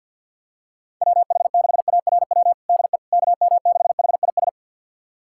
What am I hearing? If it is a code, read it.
Morse code, heard as a keyed tone, about 51 wpm, 700 Hz. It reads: WH6ARW DE KM6HIS
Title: USA Callsign DE Callsign - 6 Characters - 50wpm